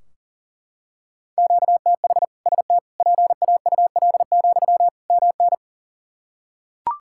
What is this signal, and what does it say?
Morse code, heard as a keyed tone, about 40 wpm, 700 Hz. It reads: QTH ST PAUL, MN
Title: K1USN SST QSO Practice 40wpm